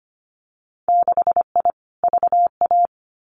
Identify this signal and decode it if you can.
Morse code, heard as a keyed tone, about 25 wpm, 700 Hz. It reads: 6I VA
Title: Winter Field Day Exchanges Encore 25wpm